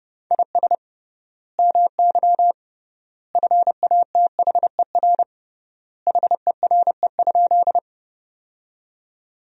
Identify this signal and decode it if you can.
Morse code, heard as a keyed tone, about 30 wpm, 700 Hz. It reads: IS MY FATHER HERE?
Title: 3x Word-Spacing - Sentences from Top 300 Words - 30wpm